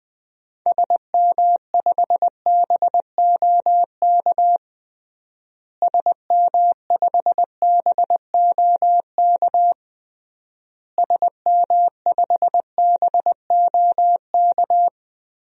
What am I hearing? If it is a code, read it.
Morse code, heard as a keyed tone, about 21 wpm, 700 Hz. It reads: SM5BOK SM5BOK SM5BOK
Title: DXCC-100 Call Signs - 6 Characters - Repeated 3x - 20wpm